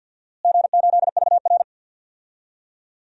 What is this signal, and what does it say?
Morse code, heard as a keyed tone, about 53 wpm, 700 Hz. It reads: G8VR
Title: DXCC-100 Call Signs - 4 Characters - Repeated 2x - 50wpm